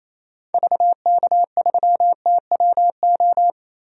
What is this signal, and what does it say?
Morse code, heard as a keyed tone, about 28 wpm, 700 Hz. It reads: VK3TWO
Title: DXCC-100 Call Signs - 6 Characters - 28wpm